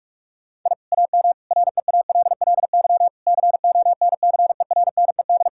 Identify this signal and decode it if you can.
Morse code, heard as a keyed tone, about 41 wpm, 700 Hz. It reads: I AM REALLY CONCERNED
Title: Sentences from Top 1000 Words - Review Part 6 - 45wpm